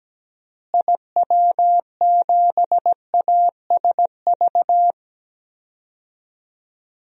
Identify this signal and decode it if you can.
Morse code, heard as a keyed tone, about 17 wpm, 700 Hz. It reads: IW7ASV
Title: DXCC-100 Call Signs - 6 Characters - Repeated 2x - 17wpm